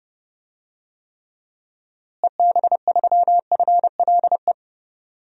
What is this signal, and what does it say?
Morse code, heard as a keyed tone, about 31 wpm, 700 Hz. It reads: EB3FLE
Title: DXCC-100 Call Signs - 6 Characters - Repeated 2x - 30wpm